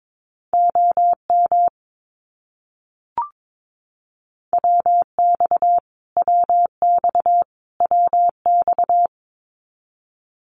Morse code, heard as a keyed tone, about 22 wpm, 700 Hz. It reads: OM WX WX WX
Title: Abbreviations Repeated 3x - 22wpm